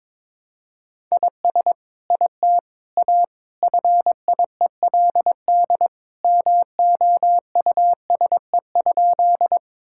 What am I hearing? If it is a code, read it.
Morse code, heard as a keyed tone, about 21 wpm, 700 Hz. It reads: IS IT A FIELD MOUSE?
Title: Sentences from Top 1000 Words - Review Part 5 - 22wpm